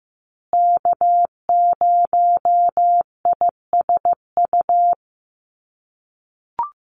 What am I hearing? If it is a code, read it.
Morse code, heard as a keyed tone, about 15 wpm, 700 Hz. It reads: K0ISU
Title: USA Call Signs Repeated 2x 15wpm